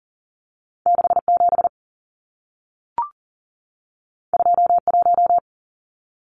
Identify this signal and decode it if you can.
Morse code, heard as a keyed tone, about 40 wpm, 700 Hz. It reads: 67 21
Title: Numbers - 2-Digit 40wpm